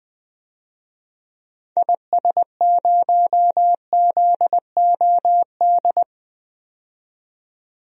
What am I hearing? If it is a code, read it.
Morse code, heard as a keyed tone, about 20 wpm, 700 Hz. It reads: IS0ZOD
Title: DXCC-100 Call Signs - 6 Characters - 20wpm